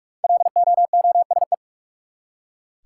Morse code, heard as a keyed tone, about 41 wpm, 700 Hz. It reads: LOOSE
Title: Five Letter Words - 45wpm